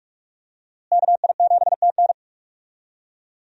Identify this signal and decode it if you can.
Morse code, heard as a keyed tone, about 46 wpm, 700 Hz. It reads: KI7TN
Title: USA Call Signs - 5 Characters - Repeated 2x - 45wpm